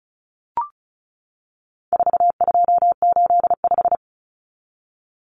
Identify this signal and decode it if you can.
Morse code, heard as a keyed tone, about 34 wpm, 700 Hz. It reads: 4285
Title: Common Cut Numbers - 4-Digit 35wpm